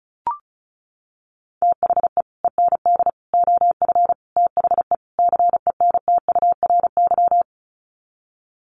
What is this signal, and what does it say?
Morse code, heard as a keyed tone, about 34 wpm, 700 Hz. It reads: THE END OF THE CENTURY
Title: Sets of 5 Words Encore2 35wpm